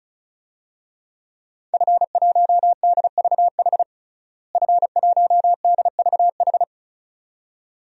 Morse code, heard as a keyed tone, about 36 wpm, 700 Hz. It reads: F1DVH F1DVH
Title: DXCC-100 Call Signs - 5 Characters - Repeated 2x - 35wpm